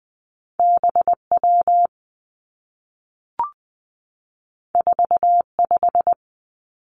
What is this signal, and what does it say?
Morse code, heard as a keyed tone, about 20 wpm, 700 Hz. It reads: B W 4 5
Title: Two Challenging Letter-Number Combinations 20wpm